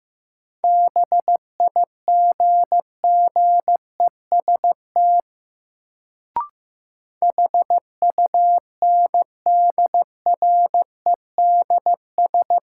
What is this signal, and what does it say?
Morse code, heard as a keyed tone, about 15 wpm, 700 Hz. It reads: BIGGEST HUNDREDS
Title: Sets of 4 Words - New Word Drill - 15wpm